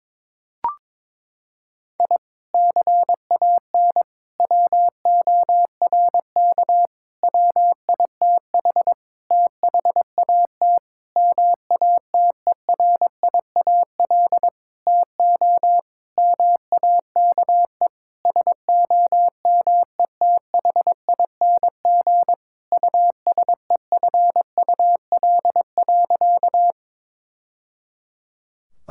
Morse code, heard as a keyed tone, about 22 wpm, 700 Hz. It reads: I CAN WORK WITH THAT MATERIAL TO MAKE SOMETHING USEFUL.
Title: Sentences from Top 1000 Words - Review Part 5 - 22wpm